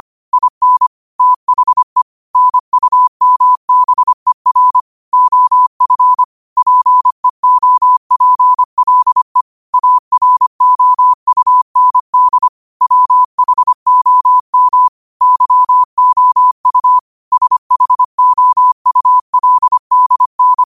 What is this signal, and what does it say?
Morse code, heard as a keyed tone, about 25 wpm, 1000 Hz. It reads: IN THE NUMBER OF PEOPLE AROUND WHOM YOU SHOULDN